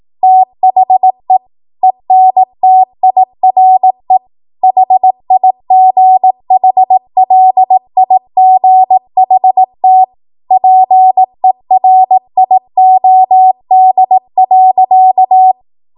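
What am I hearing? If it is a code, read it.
Morse code, heard as a keyed tone, about 18 wpm, 750 Hz. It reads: THE ENTIRE HIGHLIGHT PERIOD.
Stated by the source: ARRL 18 wpm Morse code practice archive transmission.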